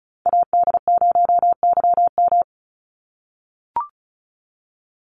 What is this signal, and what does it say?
Morse code, heard as a keyed tone, about 34 wpm, 700 Hz. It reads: AD0YM
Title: NA SOTA Chaser Callsigns 35wpm